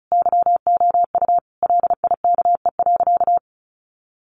Morse code, heard as a keyed tone, about 34 wpm, 700 Hz. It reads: YOU LIKE.
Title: Sentences from Top 500 Words 35wpm